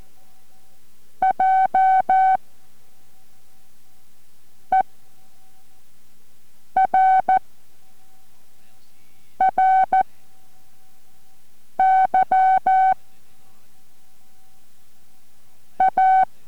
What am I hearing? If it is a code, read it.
Morse code, heard as a keyed tone, about 14 wpm, 760 Hz. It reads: J E R R Y A